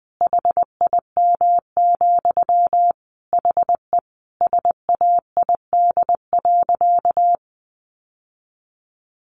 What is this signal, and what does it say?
Morse code, heard as a keyed tone, about 19 wpm, 700 Hz. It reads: HIM, HE SAID.
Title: Sentences from Top 1100 Words 20wpm